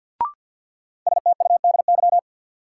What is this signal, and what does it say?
Morse code, heard as a keyed tone, about 52 wpm, 700 Hz. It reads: STUDY
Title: Sets of 5 Words - New Word Drill - 50wpm